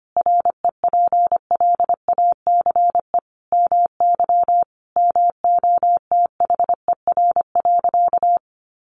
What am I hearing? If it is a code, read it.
Morse code, heard as a keyed tone, about 25 wpm, 700 Hz. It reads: REPLACE MY MOTHER.
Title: Sentences from Top 1400 Words 25wpm